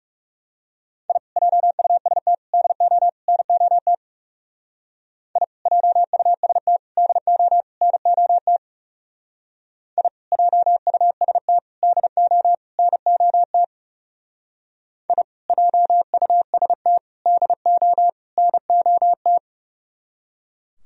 Morse code, heard as a keyed tone, about 36 wpm, 700 Hz. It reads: I JUST DO NOT I JUST DO NOT I JUST DO NOT I JUST DO NOT
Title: Speed-Racing - Sets of 4 Words 30wpm